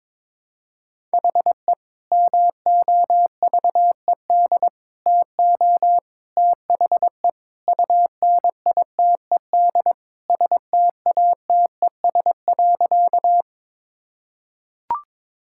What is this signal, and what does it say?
Morse code, heard as a keyed tone, about 22 wpm, 700 Hz. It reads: HE MOVED TO THE UNITED STATES.
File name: practice / 22wpm / Sentences with 3-Word Phrases - 22wpm